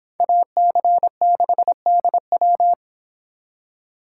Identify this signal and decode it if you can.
Morse code, heard as a keyed tone, about 26 wpm, 700 Hz. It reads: AC6DW